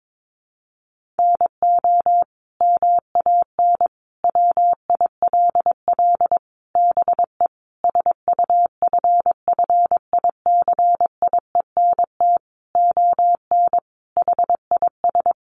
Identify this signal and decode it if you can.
Morse code, heard as a keyed tone, about 22 wpm, 700 Hz. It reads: NO MAN WILL BE SUFFICIENT ON HIS
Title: Sentences from Top 1400 Words 22wpm